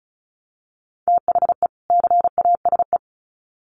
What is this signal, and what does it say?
Morse code, heard as a keyed tone, about 34 wpm, 700 Hz. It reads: THE CASE
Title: Sets of 2 Words 35wpm